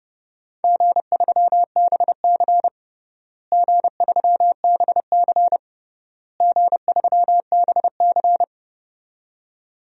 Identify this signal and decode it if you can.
Morse code, heard as a keyed tone, about 29 wpm, 700 Hz. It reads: G3BC G3BC G3BC
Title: DXCC-100 Call Signs - 4 Characters - Repeated 3x - 30wpm